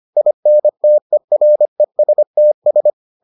Morse code, heard as a keyed tone, about 25 wpm, 600 Hz. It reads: INTERESTS